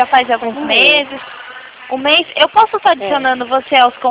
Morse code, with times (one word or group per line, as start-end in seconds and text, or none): none